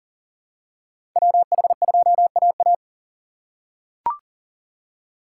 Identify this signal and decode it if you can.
Morse code, heard as a keyed tone, about 39 wpm, 700 Hz. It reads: WH2AA
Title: USA Call Signs - 5 Characters - 4x Character Spacing - 40wpm